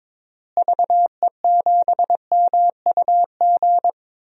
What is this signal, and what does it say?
Morse code, heard as a keyed tone, about 22 wpm, 700 Hz. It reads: VE7MUG
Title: DXCC-100 Call Signs - 6 Characters - Repeated 3x - 22wpm